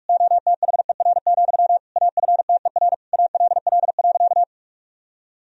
Morse code, heard as a keyed tone, about 46 wpm, 700 Hz. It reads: OTHER, AFTER ALL.
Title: Sentences from Top 1000 Words - Review Part 1 - 45wpm